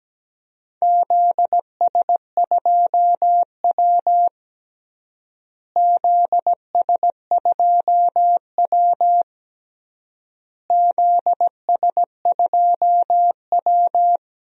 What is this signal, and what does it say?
Morse code, heard as a keyed tone, about 17 wpm, 700 Hz. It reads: ZS2W ZS2W ZS2W
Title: DXCC-100 Call Signs - 4 Characters - Repeated 3x - 17wpm